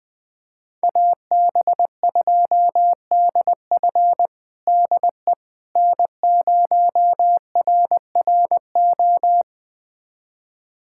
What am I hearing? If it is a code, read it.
Morse code, heard as a keyed tone, about 20 wpm, 700 Hz. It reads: AB2DF DE N0RRO
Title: USA Callsign DE Callsign 20wpm